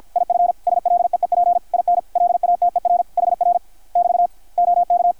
Morse code, heard as a keyed tone, about 53 wpm, 720 Hz, with radio noise. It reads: IF SLEEP IN LATER HR <BT> OK
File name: archived / CRSnet